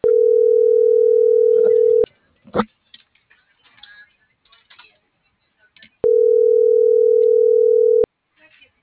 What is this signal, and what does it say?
Ringback tone (North America), 440 + 480 Hz.